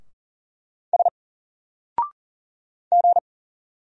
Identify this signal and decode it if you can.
Morse code, heard as a keyed tone, about 40 wpm, 700 Hz. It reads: S G
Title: Single Character (Letter:Number:Punctuation:Prosign) Alt 40wpm